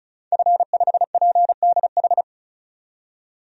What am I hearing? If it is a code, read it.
Morse code, heard as a keyed tone, about 36 wpm, 700 Hz. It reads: F5PDH
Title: DXCC-100 Call Signs - 5 Characters - 35wpm